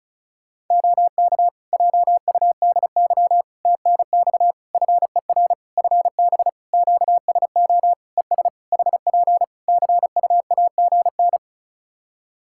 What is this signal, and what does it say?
Morse code, heard as a keyed tone, about 34 wpm, 700 Hz. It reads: OK JUDY TNX FER FB QSO ES HP CUAGN
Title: Practice QSO - Exchange 4 - Ending - 35wpm